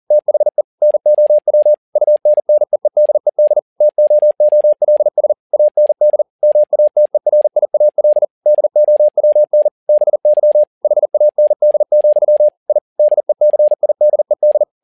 Morse code, heard as a keyed tone, about 40 wpm, 600 Hz. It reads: THE NOW UNNEEDED TOOLS AND MATERIAL DOWN BY HAND, I DECIDED
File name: learning center / tower trouble / tower touble 40wpm